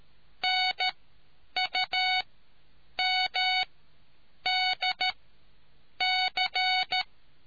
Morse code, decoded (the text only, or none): NUMDC